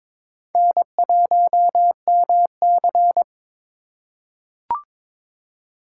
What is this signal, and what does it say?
Morse code, heard as a keyed tone, about 22 wpm, 700 Hz. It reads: N1MC
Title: USA Call Signs - 4x Character Spacing - 22wpm